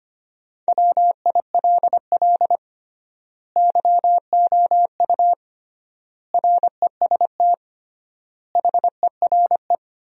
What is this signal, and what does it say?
Morse code, heard as a keyed tone, about 25 wpm, 700 Hz. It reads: WILL YOU REST HERE
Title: 3x Word-Spacing - Sentences from Top 500 Words - 25wpm